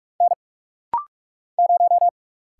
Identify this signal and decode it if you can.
Morse code, heard as a keyed tone, about 46 wpm, 700 Hz. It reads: N 0
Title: Single Letter-Number - Rapid-Fire 45wpm